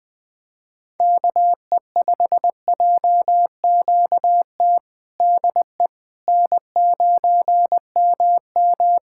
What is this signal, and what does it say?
Morse code, heard as a keyed tone, about 20 wpm, 700 Hz. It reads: KE5JQT DE N9MM
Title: K1USN SST QSO Practice 20wpm